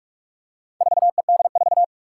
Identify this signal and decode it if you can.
Morse code, heard as a keyed tone, about 44 wpm, 700 Hz. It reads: 4ED4
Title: Four Challenging Letter-Number Combinations 45wpm